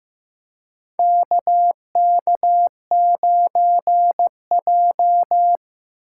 Morse code, heard as a keyed tone, about 15 wpm, 700 Hz. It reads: KK9J
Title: USA Call Signs - 4 Characters - 2x Character Spacing - 15wpm